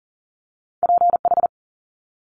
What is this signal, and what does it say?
Morse code, heard as a keyed tone, about 41 wpm, 700 Hz. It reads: PH
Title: Multi-letter Phonemes Quick Fire 40wpm